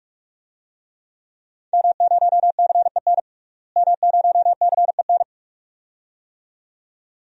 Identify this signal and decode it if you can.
Morse code, heard as a keyed tone, about 46 wpm, 700 Hz. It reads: M0CEN M0CEN
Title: DXCC-100 Call Signs - 5 Characters - Repeated 2x - 45wpm